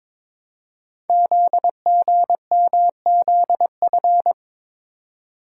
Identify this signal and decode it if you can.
Morse code, heard as a keyed tone, about 23 wpm, 700 Hz. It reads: ZGMZF
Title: Code Groups - 5 Letters - 22wpm